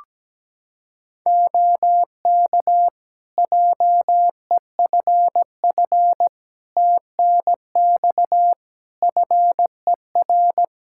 Morse code, heard as a keyed tone, about 17 wpm, 700 Hz. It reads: OK JEFF TNX FER
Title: Practice QSO - Exchange 4 - Ending - 17wpm